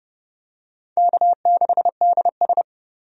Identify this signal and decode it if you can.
Morse code, heard as a keyed tone, about 30 wpm, 700 Hz. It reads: K6DS
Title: USA Call Signs - 4 Characters - 30wpm